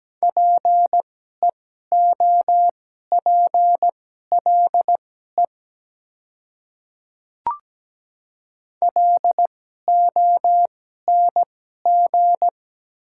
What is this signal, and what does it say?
Morse code, heard as a keyed tone, about 17 wpm, spaced as for 12 wpm, 700 Hz. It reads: PEOPLE LONG
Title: 2x Character Spacing - Top 100 Words - 17wpm